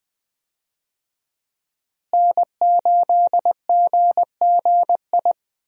Morse code, heard as a keyed tone, about 21 wpm, 700 Hz. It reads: N8GGI
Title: USA Call Signs - 5 Characters - Repeated 2x - 20wpm